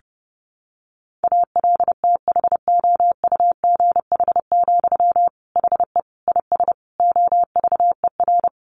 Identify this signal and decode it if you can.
Morse code, heard as a keyed tone, about 30 wpm, 700 Hz. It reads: ALTHOUGH, HE IS OVER
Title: Sentences from Top 600 Words 30wpm